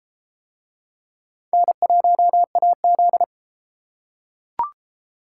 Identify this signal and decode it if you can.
Morse code, heard as a keyed tone, about 30 wpm, 700 Hz. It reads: N1AZ